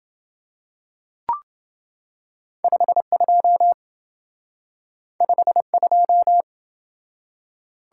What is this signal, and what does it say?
Morse code, heard as a keyed tone, about 27 wpm, 700 Hz. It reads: 52 52